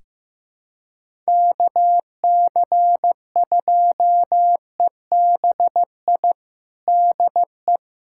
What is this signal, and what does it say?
Morse code, heard as a keyed tone, about 16 wpm, 700 Hz. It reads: KC2EBI DE
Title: USA Callsign DE Callsign 15wpm